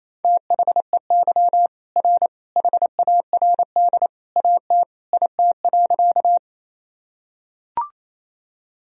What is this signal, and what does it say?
Morse code, heard as a keyed tone, about 28 wpm, 700 Hz. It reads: THEY R HARD AT IT.
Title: Sentences from Top 300 Words and Abbreviations 28wpm